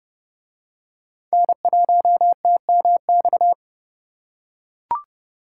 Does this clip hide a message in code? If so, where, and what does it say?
Yes: Morse code from 1.3 to 3.5 s, heard as a keyed tone, about 27 wpm, 700 Hz. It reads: N1TMX